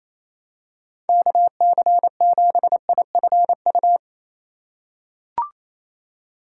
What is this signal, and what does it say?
Morse code, heard as a keyed tone, about 28 wpm, 700 Hz. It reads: KC7IFU
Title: USA Call Signs - 6 Characters - 2x Character Spacing - 28wpm